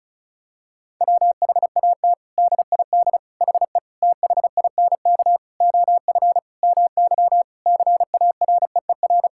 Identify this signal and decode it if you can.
Morse code, heard as a keyed tone, about 34 wpm, 700 Hz. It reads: WHAT DID HE THINK OF MY CAREER
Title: Sentences from Top 1000 Words - Review Part 6 - 35wpm